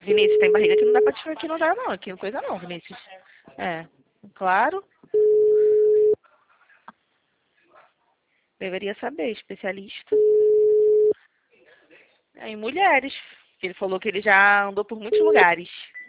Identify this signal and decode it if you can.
Ringback tone (Europe, 425 Hz), with speech.